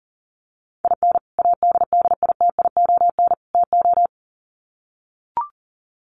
Morse code, heard as a keyed tone, about 40 wpm, 700 Hz. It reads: IN ADDITION TO
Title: Sets of 3 Words 40wpm